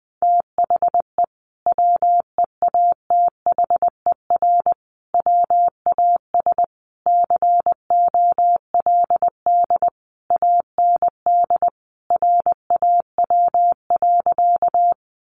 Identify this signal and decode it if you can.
Morse code, heard as a keyed tone, about 20 wpm, 700 Hz. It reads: THE WEATHER WAS COLD AND RAW.
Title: Sentences from Top 1500 Words 20wpm